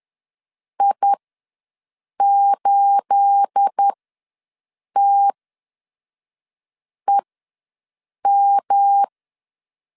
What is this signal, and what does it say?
Morse code, heard as a keyed tone, about 10 wpm, spaced as for 5 wpm, 800 Hz. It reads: I8TEM
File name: LEZIONI / 9 28 ws 10wpm 4124